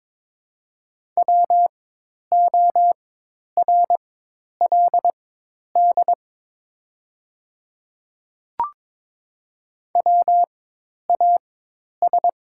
Morse code, heard as a keyed tone, about 22 wpm, spaced as for 10 wpm, 700 Hz. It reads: WORLD WAS
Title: 4x Character Spacing - Top 300 Words - 22wpm